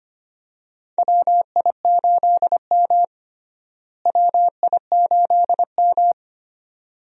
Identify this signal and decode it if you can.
Morse code, heard as a keyed tone, about 25 wpm, 700 Hz. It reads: WI8M WI8M
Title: USA Call Signs - 4 Characters - Repeated 2x - 25wpm